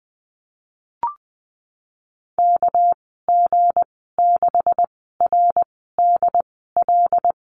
Morse code, heard as a keyed tone, about 20 wpm, spaced as for 15 wpm, 700 Hz. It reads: KG6RDL
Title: 200 Lesson 32g - Letter k - Call Signs 20wpm